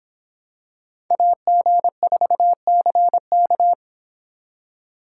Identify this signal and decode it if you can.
Morse code, heard as a keyed tone, about 26 wpm, 700 Hz. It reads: AG4CK